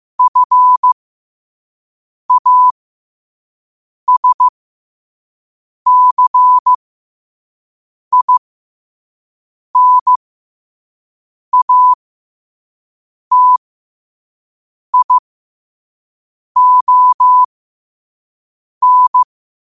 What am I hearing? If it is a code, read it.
Morse code, heard as a keyed tone, about 15 wpm, spaced as for 5 wpm, 1000 Hz. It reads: FASCINATION
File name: qotdfiles / 5wpm